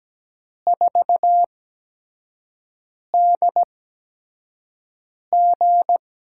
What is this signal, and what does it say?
Morse code, heard as a keyed tone, about 17 wpm, spaced as for 5 wpm, 700 Hz. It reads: 4DG
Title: USA Call Signs - 4 Characters - 8x Character Spacing - 17wpm